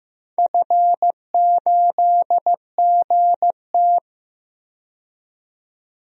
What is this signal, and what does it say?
Morse code, heard as a keyed tone, about 15 wpm, 700 Hz. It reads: F8GT
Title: DXCC-100 Call Signs - 4 Characters - 15wpm